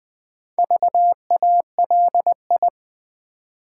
Morse code, heard as a keyed tone, about 20 wpm, 700 Hz. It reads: VALI
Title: Build A Word - 4 Letters At A Time - 8-Letter Words - 20wpm